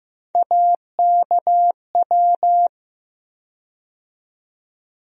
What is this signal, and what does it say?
Morse code, heard as a keyed tone, about 15 wpm, 700 Hz. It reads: AKW
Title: USA Call Sign Postfixes - 3 Letters 15wpm